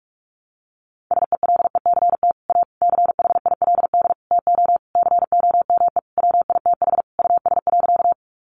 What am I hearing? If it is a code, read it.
Morse code, heard as a keyed tone, about 44 wpm, 700 Hz. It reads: SELECT A CHILD TO COME WITH US.
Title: Sentences from Top 1300 Words 45wpm